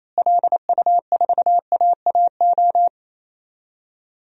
Morse code, heard as a keyed tone, about 27 wpm, 700 Hz. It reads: LU4AAO
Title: DXCC-100 Call Signs - 6 Characters - Repeated 3x - 28wpm